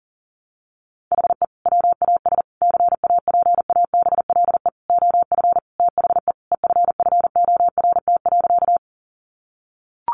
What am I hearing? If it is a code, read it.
Morse code, heard as a keyed tone, about 40 wpm, 700 Hz. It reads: HE WAS CAPABLE OF THE EFFORT.
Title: Sentences from Top 1200 Words 40wpm